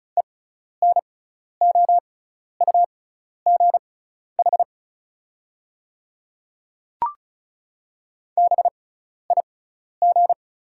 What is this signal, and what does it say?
Morse code, heard as a keyed tone, about 34 wpm, spaced as for 12 wpm, 690 Hz. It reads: ENOUGH BIG
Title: 6x Character Spacing - Top 500 Words - 35wpm